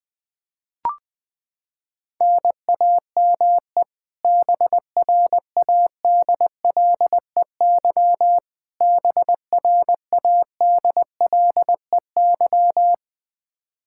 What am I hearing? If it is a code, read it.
Morse code, heard as a keyed tone, about 21 wpm, 700 Hz. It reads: NAME BRADLEY BRADLEY
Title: Practice QSO - Exchange 1 - Beginning - 20wpm